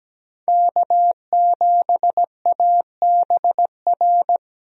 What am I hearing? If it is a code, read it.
Morse code, heard as a keyed tone, about 17 wpm, 700 Hz. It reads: K7ABR
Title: USA Call Signs - 5 Characters - 17wpm